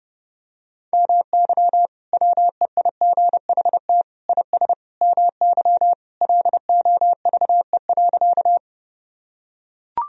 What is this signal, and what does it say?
Morse code, heard as a keyed tone, about 32 wpm, 700 Hz. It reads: MY WEIGHT IS MY LOVE.
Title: Sentences from Top 1000 Words - Review Part 3 - 30wpm